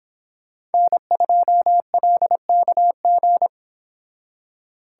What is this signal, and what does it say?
Morse code, heard as a keyed tone, about 30 wpm, 700 Hz. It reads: N2LKG